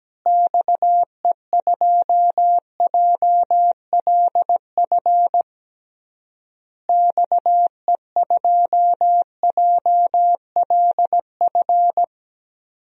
Morse code, heard as a keyed tone, about 17 wpm, 700 Hz. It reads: XE2JLF XE2JLF
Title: DXCC-100 Call Signs - 6 Characters - Repeated 3x - 17wpm